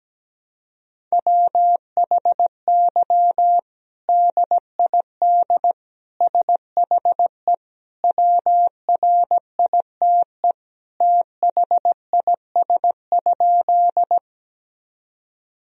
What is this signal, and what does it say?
Morse code, heard as a keyed tone, about 17 wpm, 700 Hz. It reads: WHY DID SHE WRITE THIS?
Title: Sentences from Top 1000 Words - Review Part 1 - 17wpm